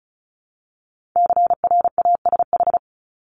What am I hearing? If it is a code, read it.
Morse code, heard as a keyed tone, about 36 wpm, 700 Hz. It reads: CRASH
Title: Multi-letter Phonemes 35wpm